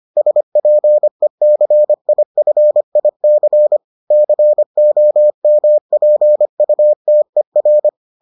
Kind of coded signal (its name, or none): Morse code